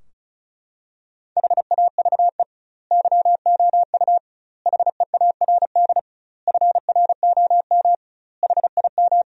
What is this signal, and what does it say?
Morse code, heard as a keyed tone, about 36 wpm, 700 Hz. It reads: HAVE YOU HEARD FROM HIM
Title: 2x Word-Spacing - Sentences from Top 400 Words - 35wpm